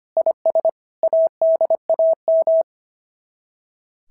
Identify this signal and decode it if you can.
Morse code, heard as a keyed tone, about 25 wpm, 690 Hz. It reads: IS ADAM
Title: SOTA QSO Practice 25wpm